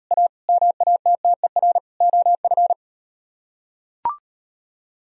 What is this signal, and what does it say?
Morse code, heard as a keyed tone, about 36 wpm, 700 Hz. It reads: A MATTER OF